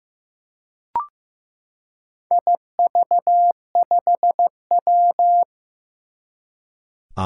Morse code, heard as a keyed tone, about 15 wpm, 700 Hz. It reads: IV5W
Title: Four Challenging Letter-Number Combinations 15wpm